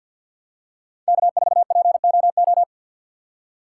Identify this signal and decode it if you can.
Morse code, heard as a keyed tone, about 52 wpm, 700 Hz. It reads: K4POO
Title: USA Call Signs - 5 Characters - 3x Character Spacing - 50wpm